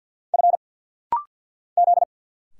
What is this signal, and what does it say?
Morse code, heard as a keyed tone, about 50 wpm, 700 Hz. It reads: F Z
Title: Single Letter-Number Alt - Rapid-Fire 50wpm